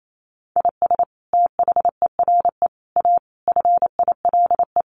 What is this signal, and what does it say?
Morse code, heard as a keyed tone, about 28 wpm, 700 Hz. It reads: IS THERE A FILE
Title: Sentences from Top 1200 Words 28wpm